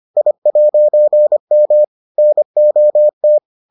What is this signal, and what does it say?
Morse code, heard as a keyed tone, about 25 wpm, 600 Hz. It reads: I'M NOT